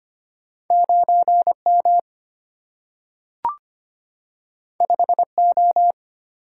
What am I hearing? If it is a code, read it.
Morse code, heard as a keyed tone, about 25 wpm, 700 Hz. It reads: 9M 5O
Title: Single-Letter Single-Number 25wpm